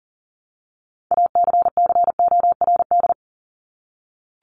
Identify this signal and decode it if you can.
Morse code, heard as a keyed tone, about 40 wpm, 700 Hz. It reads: ACCORD
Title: ING Suffix - Top 100 Words with Base Words 40wpm